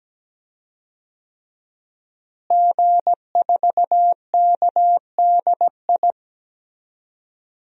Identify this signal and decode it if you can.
Morse code, heard as a keyed tone, about 17 wpm, 700 Hz. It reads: G4KDI
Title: DXCC-100 Call Signs - 5 Characters - Repeated 3x - 17wpm